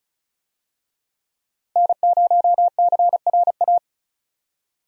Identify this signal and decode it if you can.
Morse code, heard as a keyed tone, about 36 wpm, 700 Hz. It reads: N0CRA
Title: USA Call Signs - 5 Characters - 35wpm